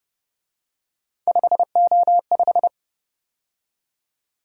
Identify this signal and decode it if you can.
Morse code, heard as a keyed tone, about 29 wpm, 700 Hz. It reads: 5O5
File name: Speed-Racing - Three Letter-Number Combinations - 25wpm